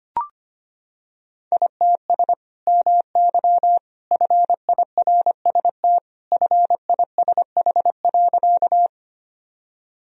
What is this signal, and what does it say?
Morse code, heard as a keyed tone, about 25 wpm, 700 Hz. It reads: ITS MY FIRST FISH.
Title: Sentences from Top 1000 Words - Review Part 1 - 25wpm